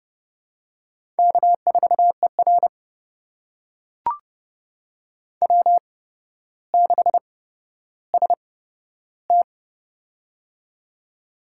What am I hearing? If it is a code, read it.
Morse code, heard as a keyed tone, about 32 wpm, spaced as for 13 wpm, 700 Hz. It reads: K4ER W6ST
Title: USA Call Signs - 4 Characters - 8x Character Spacing - 30wpm